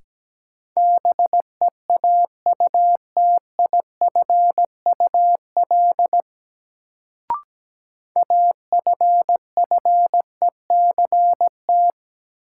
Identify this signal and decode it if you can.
Morse code, heard as a keyed tone, about 17 wpm, 700 Hz. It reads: BEAUTIFUL AFFECT
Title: Sentences from Top 900 Words - New Word Drill - 17wpm